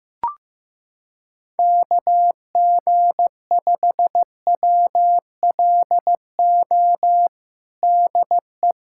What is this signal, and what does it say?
Morse code, heard as a keyed tone, about 15 wpm, 700 Hz. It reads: KG5WLO DE
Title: USA Callsign DE Callsign 15wpm